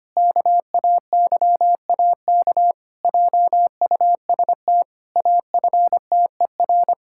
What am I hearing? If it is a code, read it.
Morse code, heard as a keyed tone, about 25 wpm, 700 Hz. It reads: KAYAK JUST AFTER
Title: Pangram Challenge v2 - 25wpm